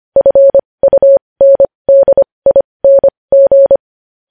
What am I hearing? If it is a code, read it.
Morse code, heard as a keyed tone, about 25 wpm, spaced as for 20 wpm, 550 Hz. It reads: FUNDING
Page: news-headlines-in-morse-code-at-20-wpm